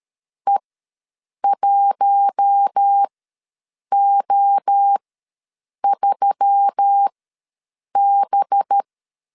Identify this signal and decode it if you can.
Morse code, heard as a keyed tone, about 13 wpm, spaced as for 7 wpm, 800 Hz. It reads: E1O3B
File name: LEZIONI / 5 LBVC / 5 LBV ws 12wpm 3015